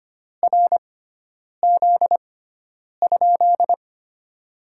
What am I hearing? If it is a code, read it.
Morse code, heard as a keyed tone, about 25 wpm, spaced as for 9 wpm, 700 Hz. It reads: RZ?
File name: Top 90 QSO Elements - 6x Character Spacing - 25wpm